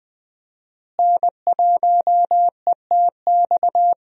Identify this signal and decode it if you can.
Morse code, heard as a keyed tone, about 20 wpm, 700 Hz. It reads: N1ETX
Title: USA Call Signs - 5 Characters - Repeated 3x - 20wpm